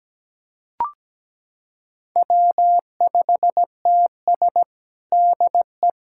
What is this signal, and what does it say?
Morse code, heard as a keyed tone, about 17 wpm, 700 Hz. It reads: W5TS DE
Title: USA Callsign DE Callsign - 4 Characters - 17wpm